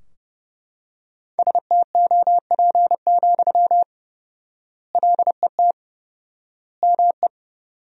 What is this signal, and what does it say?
Morse code, heard as a keyed tone, about 30 wpm, 700 Hz. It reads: STOP, LET ME
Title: 4x Word-Spacing - Sentences from Top 300 Words - 30wpm